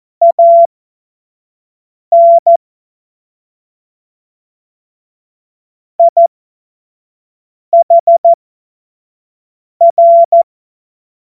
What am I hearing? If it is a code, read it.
Morse code, heard as a keyed tone, about 14 wpm, spaced as for 5 wpm, 680 Hz. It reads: AN IHR
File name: Morsekurs / Witze / Witz 09 April 2024 25 F70